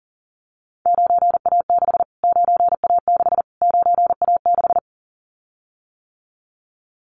Morse code, heard as a keyed tone, about 40 wpm, 700 Hz. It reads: 9A6 9A6 9A6
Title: Callsign Prefixes for Top 25 DXCC - Repeated 3x - 40wpm